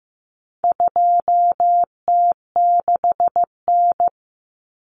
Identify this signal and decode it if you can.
Morse code, heard as a keyed tone, about 15 wpm, 700 Hz. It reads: 2T6N
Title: Common Cut Numbers - 4-Digit 15wpm